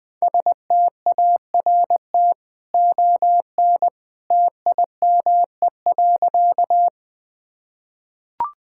Morse code, heard as a keyed tone, about 20 wpm, 700 Hz. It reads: START ON TIME.
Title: Easy Sentences with Hints - 20wpm